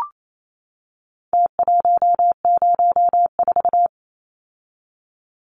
Morse code, heard as a keyed tone, about 28 wpm, 700 Hz. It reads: T104
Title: Numbers - Contest Style - 4-Digit 28wpm